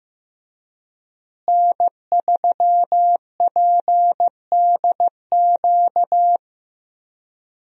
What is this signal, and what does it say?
Morse code, heard as a keyed tone, about 15 wpm, 700 Hz. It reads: N3PDQ
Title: USA Call Signs - 5 Characters - Repeated 2x - 15wpm